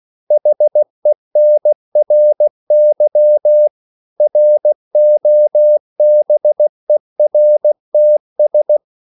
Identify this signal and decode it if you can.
Morse code, heard as a keyed tone, about 16 wpm, 600 Hz. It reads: HENRY ROBERTS
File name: CW / Practice / EnglishQuotations0416